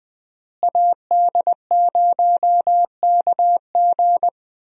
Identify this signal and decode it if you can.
Morse code, heard as a keyed tone, about 20 wpm, 700 Hz. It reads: AD0KG
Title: USA Call Signs - 5 Characters - Repeated 3x - 20wpm